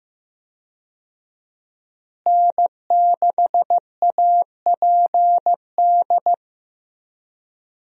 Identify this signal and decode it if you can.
Morse code, heard as a keyed tone, about 15 wpm, 700 Hz. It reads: N6APD
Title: USA Call Signs - 5 Characters - Repeated 2x - 15wpm